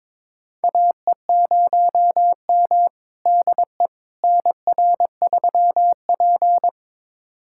Morse code, heard as a keyed tone, about 22 wpm, 700 Hz. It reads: AE0M DE NR3P
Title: USA Callsign DE Callsign - 4 Characters - 22wpm